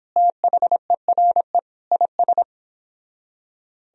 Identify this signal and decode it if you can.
Morse code, heard as a keyed tone, about 31 wpm, 700 Hz. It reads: THERE IS